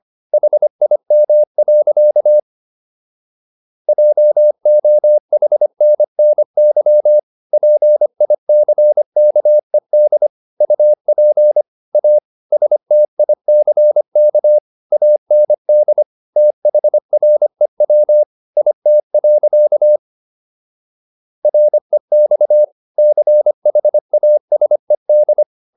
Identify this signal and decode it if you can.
Morse code, heard as a keyed tone, about 25 wpm, 600 Hz. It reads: HIM. JOHNNY PICKED UP A STICK AND THREW IT. REX CHASED